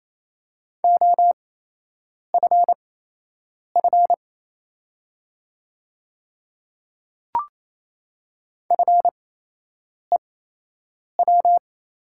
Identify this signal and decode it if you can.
Morse code, heard as a keyed tone, about 27 wpm, spaced as for 8 wpm, 690 Hz. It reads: OFF FEW
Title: 8x Character Spacing - Top 300 Words - 28wpm